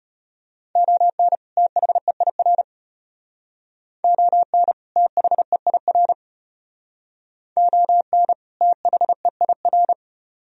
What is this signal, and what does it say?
Morse code, heard as a keyed tone, about 33 wpm, 700 Hz. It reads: ON THEIR ON THEIR ON THEIR